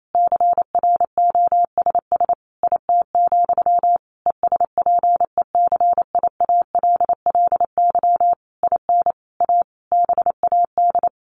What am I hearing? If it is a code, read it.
Morse code, heard as a keyed tone, about 28 wpm, 700 Hz. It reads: CROSS IT, ESPECIALLY IN A BAD
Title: Sentences from Top 1400 Words 28wpm